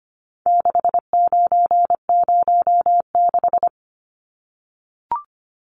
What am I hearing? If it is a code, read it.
Morse code, heard as a keyed tone, about 25 wpm, 700 Hz. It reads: 6906
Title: Numbers - 4-Digit 25wpm